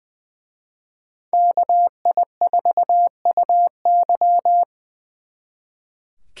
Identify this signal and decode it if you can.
Morse code, heard as a keyed tone, about 24 wpm, 700 Hz. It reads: KI4UY
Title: Speed Racing - USA 2x2 Call Signs 20wpm